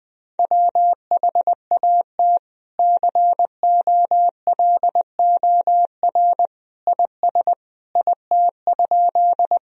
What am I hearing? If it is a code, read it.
Morse code, heard as a keyed tone, about 20 wpm, 700 Hz. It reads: WHAT COLOR IS IT?
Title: Sentences from Top 1000 Words - Review Part 2 - 20wpm